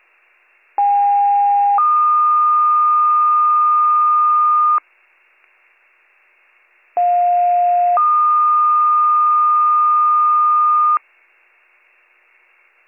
Two-tone page, heard as a one-second tone, then a three-second tone, about 810 then 1220 Hz, about 710 then 1220 Hz. It is heard from 0.8 to 11.0 s.